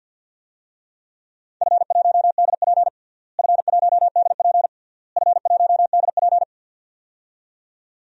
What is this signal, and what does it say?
Morse code, heard as a keyed tone, about 47 wpm, 700 Hz. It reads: F1DP F1DP F1DP
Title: DXCC-100 Call Signs - 4 Characters - Repeated 3x - 50wpm